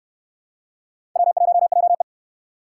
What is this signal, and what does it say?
Morse code, heard as a keyed tone, about 68 wpm, 700 Hz. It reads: W1JE